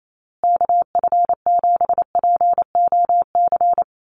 Morse code, heard as a keyed tone, about 28 wpm, 700 Hz. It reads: KF7POC
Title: USA Call Signs Repeated 2x 28wpm